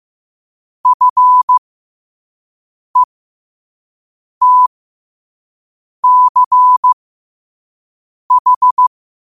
Morse code, heard as a keyed tone, about 15 wpm, spaced as for 5 wpm, 1000 Hz. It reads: FETCH
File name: qotdfiles / 5wpm